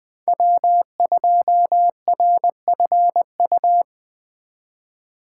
Morse code, heard as a keyed tone, about 20 wpm, 700 Hz. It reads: W2RFU
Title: USA Call Signs - 5 Characters - 2x Character Spacing - 20wpm